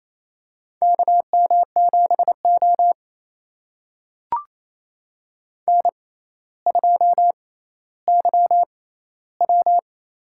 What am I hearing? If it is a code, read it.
Morse code, heard as a keyed tone, about 28 wpm, spaced as for 14 wpm, 700 Hz. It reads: KM7O N2YW
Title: USA Call Signs - 4 Characters - 6x Character Spacing - 28wpm